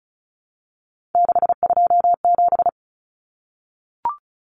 Morse code, heard as a keyed tone, about 34 wpm, 700 Hz. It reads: 627
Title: Common Cut Numbers - 3-Digit 35wpm